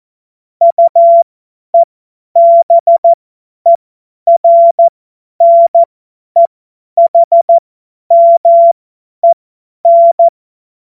Morse code, heard as a keyed tone, about 14 wpm, spaced as for 10 wpm, 680 Hz. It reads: UEBERNEHMEN
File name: Morsekurs / Witze / Witz 10. Mai 2022 50 F70